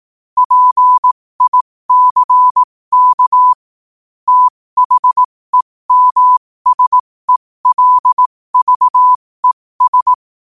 Morse code, heard as a keyed tone, about 18 wpm, spaced as for 15 wpm, 1000 Hz. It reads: PICK THEMSELVES
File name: qotdfiles / 15wpm